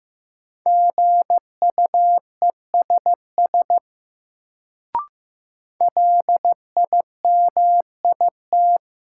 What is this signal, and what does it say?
Morse code, heard as a keyed tone, about 15 wpm, 700 Hz. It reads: GUESS LIMIT
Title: Sentences from Top 1500 Words - New Word Drill Alt - 15wpm